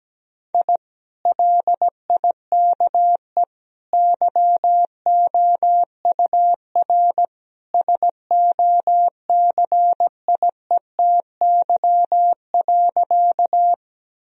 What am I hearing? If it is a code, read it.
Morse code, heard as a keyed tone, about 17 wpm, 700 Hz. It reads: I LIKE YOUR SOCIETY.
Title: Sentences from Top 1000 Words - Review Part 4 - 17wpm